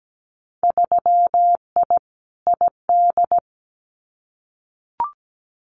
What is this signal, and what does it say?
Morse code, heard as a keyed tone, about 17 wpm, 700 Hz. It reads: 3I ID
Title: Winter Field Day Exchanges Encore 17wpm